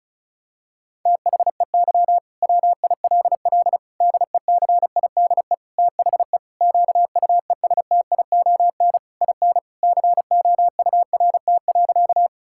Morse code, heard as a keyed tone, about 36 wpm, 700 Hz. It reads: THEY WILL DECIDE THE QUESTION IN COURT.
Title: Sentences from Top 1600 Words 35wpm